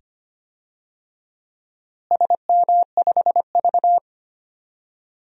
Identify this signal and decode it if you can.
Morse code, heard as a keyed tone, about 25 wpm, 700 Hz. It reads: SM5V
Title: DXCC-100 Call Signs - 4 Characters - Repeated 2x - 25wpm